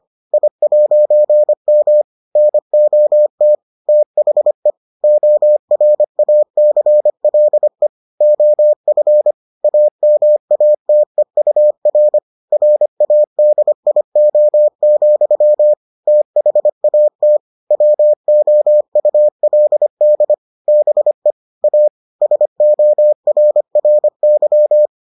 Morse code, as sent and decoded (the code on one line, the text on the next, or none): .. .----. -- / -. --- - / - .... / --- .-. .- -.-. .-.. / --- ..-. / .- -- .- - . ..- .-. / .-. .- -.. .. --- --..-- / - .... .- - / .-- --- ..- .-.. -.. / -... / .- / ... --- .-. .-. -.--
I'M NOT THE ORACLE OF AMATEUR RADIO, THAT WOULD BE A SORRY